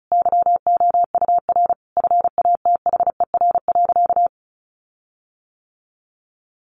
Morse code, heard as a keyed tone, about 34 wpm, 700 Hz. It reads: YOUR FATHER.
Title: Sentences from Top 1400 Words 35wpm